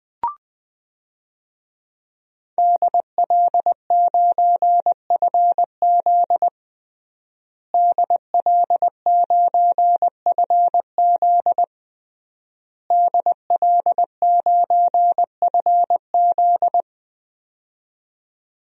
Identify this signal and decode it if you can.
Morse code, heard as a keyed tone, about 20 wpm, 700 Hz. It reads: DL9FZ DL9FZ DL9FZ
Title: DXCC-100 Call Signs - 5 Characters - Repeated 3x - 20wpm